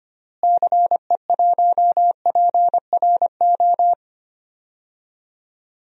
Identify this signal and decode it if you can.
Morse code, heard as a keyed tone, about 25 wpm, 700 Hz. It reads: CE1PRO
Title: DXCC-100 Call Signs - 6 Characters - Repeated 2x - 25wpm